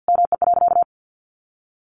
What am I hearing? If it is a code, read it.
Morse code, heard as a keyed tone, about 49 wpm, 700 Hz. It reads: ME.
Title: Sentences from Top 1100 Words 50wpm